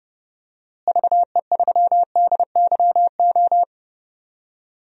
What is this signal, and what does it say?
Morse code, heard as a keyed tone, about 30 wpm, 700 Hz. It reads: VE3DYO
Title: DXCC-100 Call Signs - 6 Characters - Repeated 3x - 30wpm